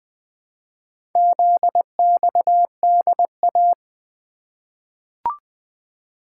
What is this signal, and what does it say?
Morse code, heard as a keyed tone, about 20 wpm, 700 Hz. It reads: ZXDA
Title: Four Letter-Number Combinations - 20wpm